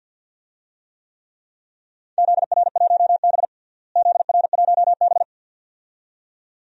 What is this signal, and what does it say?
Morse code, heard as a keyed tone, about 49 wpm, 700 Hz. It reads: ZR1B ZR1B
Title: DXCC-100 Call Signs - 4 Characters - Repeated 2x - 50wpm